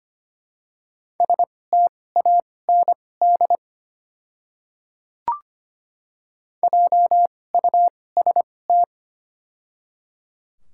Morse code, heard as a keyed tone, about 25 wpm, spaced as for 18 wpm, 700 Hz. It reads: STAND JUST
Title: 2x Character Spacing - Top 300 Words - 25wpm